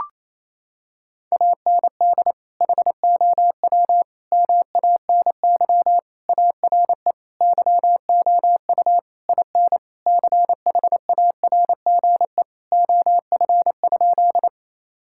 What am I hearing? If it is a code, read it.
Morse code, heard as a keyed tone, about 27 wpm, 700 Hz. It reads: AND HOW MANY ARE YOU IN CHARGE OF?
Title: Sentences from Top 1000 Words - Review Part 6 - 28wpm